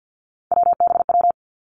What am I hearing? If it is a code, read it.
Morse code, heard as a keyed tone, about 50 wpm, 700 Hz. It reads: WBW
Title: Three Challenging Letter-Number Combinations 50wpm